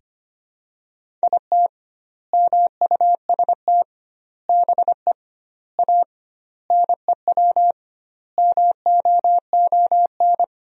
Morse code, heard as a keyed tone, about 25 wpm, 700 Hz. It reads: IT MUST BE A NEW MOON
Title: 2x Word-Spacing - Sentences from Top 500 Words - 25wpm